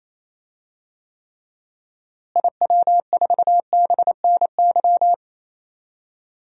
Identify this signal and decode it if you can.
Morse code, heard as a keyed tone, about 27 wpm, 700 Hz. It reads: IW4BNY
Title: DXCC-100 Call Signs - 6 Characters - Repeated 3x - 28wpm